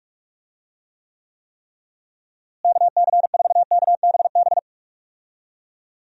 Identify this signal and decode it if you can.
Morse code, heard as a keyed tone, about 46 wpm, 700 Hz. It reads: KC4KBB
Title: USA Call Signs - 6 Characters - Repeated 3x - 45wpm